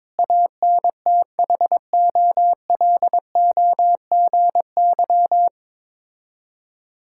Morse code, heard as a keyed tone, about 22 wpm, 700 Hz. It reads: ANTHOLOGY
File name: Difficult Words with Hints - 22wpm